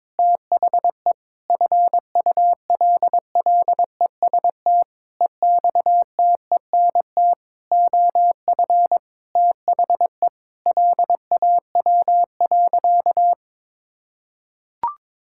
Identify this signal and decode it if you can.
Morse code, heard as a keyed tone, about 21 wpm, 700 Hz. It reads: THE FULLEST EXTENT OF THE LAW.
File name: Sentences with 7-Word Phrases - 22wpm